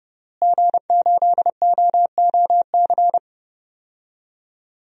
Morse code, heard as a keyed tone, about 30 wpm, 700 Hz. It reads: G8OOC
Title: DXCC-100 Call Signs - 5 Characters - Repeated 3x - 30wpm